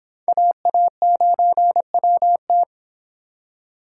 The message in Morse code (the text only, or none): AA9WT